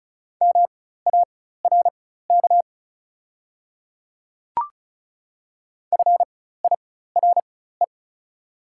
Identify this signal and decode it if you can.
Morse code, heard as a keyed tone, about 36 wpm, spaced as for 16 wpm, 700 Hz. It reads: MARK FIRE
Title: 4x Character Spacing - Top 500 Words - 35wpm